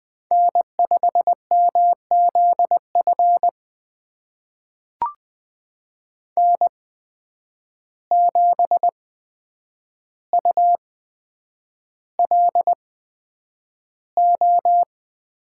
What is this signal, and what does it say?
Morse code, heard as a keyed tone, about 20 wpm, spaced as for 9 wpm, 700 Hz. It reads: N5MZF N7ULO
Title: USA Call Signs - 5 Characters - 8x Character Spacing - 20wpm